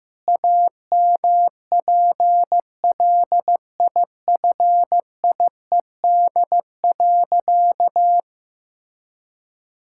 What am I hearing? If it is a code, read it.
Morse code, heard as a keyed tone, about 15 wpm, 700 Hz. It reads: AMPLIFIED.
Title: Pangram Challenge v2 - 15wpm